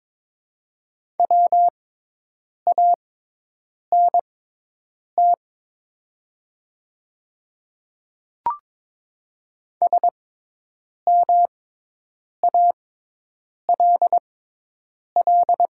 Morse code, heard as a keyed tone, about 22 wpm, spaced as for 8 wpm, 700 Hz. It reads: WANT SMALL
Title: 6x Character Spacing - Top 300 Words - 22wpm